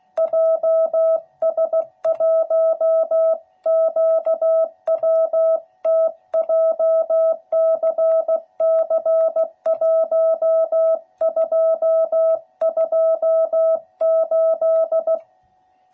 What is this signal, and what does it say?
Morse code, heard as a keyed tone, about 16 wpm, 660 Hz. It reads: JS1QWT JCC1228
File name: jh8cbh / 2021cw / cw12b